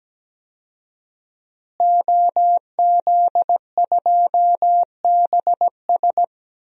Morse code, heard as a keyed tone, about 17 wpm, 700 Hz. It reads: OZ2BS
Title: DXCC-100 Call Signs - 5 Characters - Repeated 2x - 17wpm